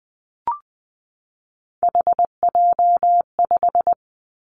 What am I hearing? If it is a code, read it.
Morse code, heard as a keyed tone, about 20 wpm, 700 Hz. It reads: HJ5
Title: Callsign Prefixes for Top 100 DXCC - 20wpm